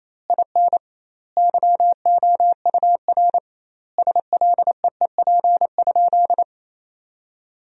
Morse code, heard as a keyed tone, about 28 wpm, 700 Hz. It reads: IN YOUR SLEEP?
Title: 2x Word-Spacing - Sentences from Top 500 Words - 28wpm